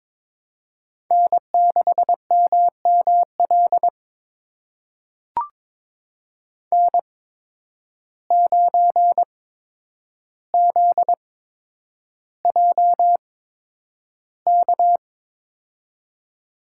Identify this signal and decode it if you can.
Morse code, heard as a keyed tone, about 22 wpm, spaced as for 9 wpm, 700 Hz. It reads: N6MML N9ZJK
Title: USA Call Signs - 5 Characters - 8x Character Spacing - 22wpm